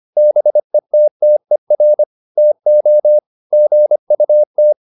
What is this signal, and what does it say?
Morse code, heard as a keyed tone, about 25 wpm, 600 Hz. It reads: BETTER TO GUT